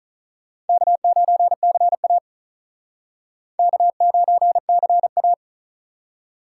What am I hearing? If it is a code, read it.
Morse code, heard as a keyed tone, about 43 wpm, 700 Hz. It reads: K9CA K9CA
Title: Speed Racing - USA Call Signs 35wpm